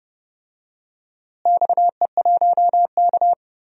Morse code, heard as a keyed tone, about 30 wpm, 700 Hz. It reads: XE1K
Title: DXCC-100 Call Signs - 4 Characters - Repeated 2x - 30wpm